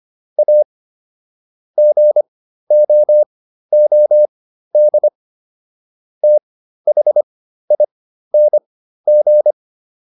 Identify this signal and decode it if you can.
Morse code, heard as a keyed tone, about 25 wpm, spaced as for 13 wpm, 600 Hz. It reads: A GOOD THING